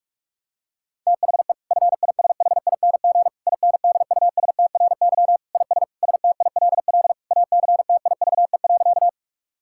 Morse code, heard as a keyed tone, about 46 wpm, 700 Hz. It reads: THE FISHING INDUSTRY IS STILL ACTIVE.
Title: Sentences from Top 1000 Words - Review Part 6 - 45wpm